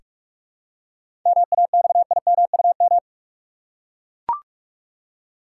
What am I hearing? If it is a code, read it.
Morse code, heard as a keyed tone, about 42 wpm, 700 Hz. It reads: MAXIMUM
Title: Seven Letter Words - 45wpm